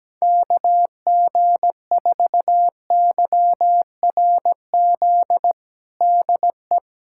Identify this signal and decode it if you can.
Morse code, heard as a keyed tone, about 17 wpm, 700 Hz. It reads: KG4YRZ DE
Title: USA Callsign DE Callsign - 6 Characters - 17wpm